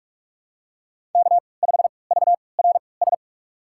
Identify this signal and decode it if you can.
Morse code, heard as a keyed tone, about 46 wpm, spaced as for 26 wpm, 700 Hz. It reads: K5VRS
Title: USA Call Signs - 5 Characters - 3x Character Spacing - 45wpm